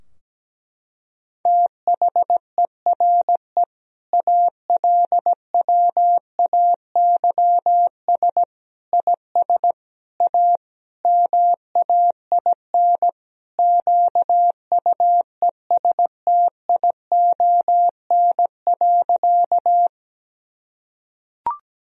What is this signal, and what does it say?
Morse code, heard as a keyed tone, about 17 wpm, 700 Hz. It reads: THERE ALWAYS IS A MAIN QUESTION.
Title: Sentences from Top 1000 Words - Review Part 2 - 17wpm